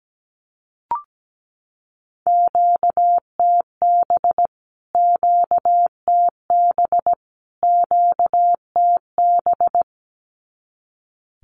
Morse code, heard as a keyed tone, about 17 wpm, 700 Hz. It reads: QTB QTB QTB
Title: Q-Codes Repeated 3x 17wpm